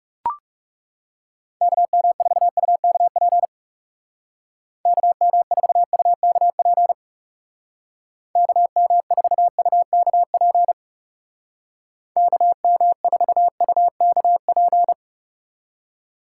Morse code, heard as a keyed tone, about 38 wpm, 700 Hz. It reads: KM4UKP KM4UKP KM4UKP KM4UKP
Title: Speed Racing - USA 2x3 Call Signs 30wpm